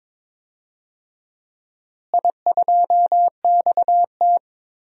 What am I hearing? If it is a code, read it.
Morse code, heard as a keyed tone, about 22 wpm, 700 Hz. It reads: I2XT
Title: DXCC-100 Call Signs - 4 Characters - Repeated 2x - 22wpm